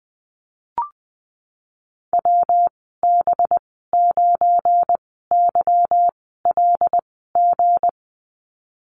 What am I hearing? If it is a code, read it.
Morse code, heard as a keyed tone, about 20 wpm, spaced as for 15 wpm, 700 Hz. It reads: WB9YLG
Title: 193 Lesson 31g - Letter v - Call Signs 20wpm